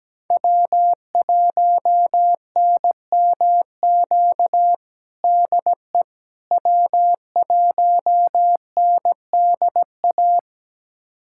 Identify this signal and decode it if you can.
Morse code, heard as a keyed tone, about 17 wpm, 700 Hz. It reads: W1NMQ DE W1NDA
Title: USA Callsign DE Callsign 17wpm